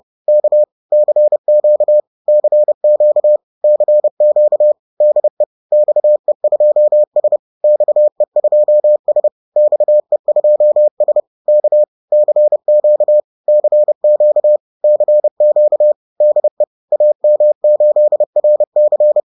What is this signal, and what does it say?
Morse code, heard as a keyed tone, about 30 wpm, 600 Hz. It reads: K CQ CQ CQ DE XE2S XE2S XE2S K CQ CQ CQ DE AM8RC
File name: 30wpm-7